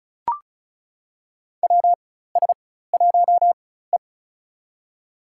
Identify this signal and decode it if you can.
Morse code, heard as a keyed tone, about 36 wpm, spaced as for 16 wpm, 700 Hz. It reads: WS1E
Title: USA Call Signs - 4 Characters - 4x Character Spacing - 35wpm